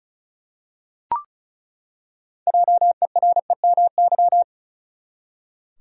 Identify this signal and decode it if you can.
Morse code, heard as a keyed tone, about 36 wpm, 700 Hz. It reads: JEREMY
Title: USA Common Names - 35wpm